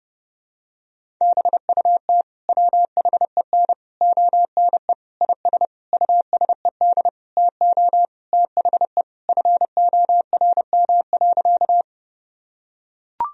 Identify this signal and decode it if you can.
Morse code, heard as a keyed tone, about 32 wpm, 700 Hz. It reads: BUT WHEN ONE IS USED TO THE FORM.
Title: Sentences from Top 1000 Words - Review Part 1 - 30wpm